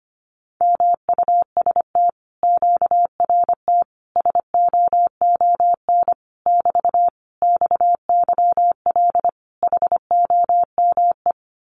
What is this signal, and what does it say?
Morse code, heard as a keyed tone, about 25 wpm, 700 Hz. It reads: MUST QRT SOON <BT> XYL HOME
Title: Common QSO Phrases Encore 25wpm